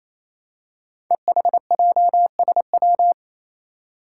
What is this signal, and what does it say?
Morse code, heard as a keyed tone, about 27 wpm, 700 Hz. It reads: EHJSW
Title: Code Groups - 5 Letters - 28wpm